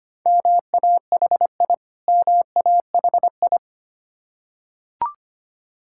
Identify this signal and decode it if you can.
Morse code, heard as a keyed tone, about 25 wpm, 700 Hz. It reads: MAHI MAHI
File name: practice / 25wpm / Difficult Words with Hints - 25wpm